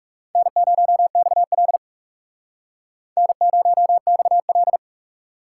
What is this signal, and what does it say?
Morse code, heard as a keyed tone, about 37 wpm, 700 Hz. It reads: N0XL N0XL
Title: Speed Racing - USA Call Signs 30wpm